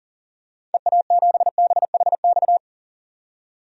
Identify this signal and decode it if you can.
Morse code, heard as a keyed tone, about 40 wpm, 700 Hz. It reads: EA7BHX
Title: DXCC-100 Call Signs - 6 Characters - 40wpm